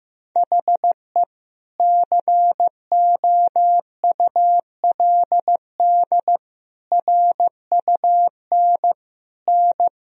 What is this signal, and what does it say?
Morse code, heard as a keyed tone, about 15 wpm, 700 Hz. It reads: HE COULD RUN N
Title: Sentences from Top 300 Words and Abbreviations 15wpm